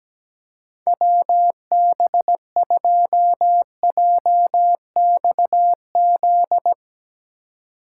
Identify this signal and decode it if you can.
Morse code, heard as a keyed tone, about 17 wpm, 700 Hz. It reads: WB2JXZ
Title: USA Call Signs - 6 Characters - 3x Character Spacing - 17wpm